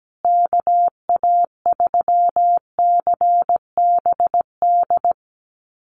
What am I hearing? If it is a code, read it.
Morse code, heard as a keyed tone, about 17 wpm, 700 Hz. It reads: KA3CBD
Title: USA Call Signs Repeated 3x 17wpm